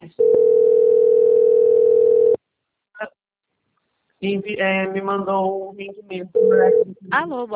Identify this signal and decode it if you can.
Ringback tone (North America), 440 + 480 Hz, with speech.
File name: out- - - - - .94528